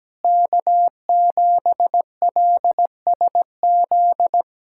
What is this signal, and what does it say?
Morse code, heard as a keyed tone, about 17 wpm, 700 Hz. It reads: K7LSZ
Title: USA Callsign DE Callsign - 5 Characters - 17wpm